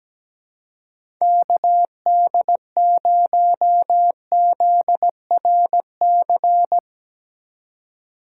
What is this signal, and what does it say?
Morse code, heard as a keyed tone, about 17 wpm, 700 Hz. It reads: KD0ZRC
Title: USA Call Signs - 6 Characters - 3x Character Spacing - 17wpm